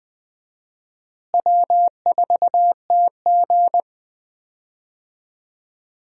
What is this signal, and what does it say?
Morse code, heard as a keyed tone, about 20 wpm, 700 Hz. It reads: W4TG
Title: USA Call Signs - 4 Characters - Repeated 3x - 20wpm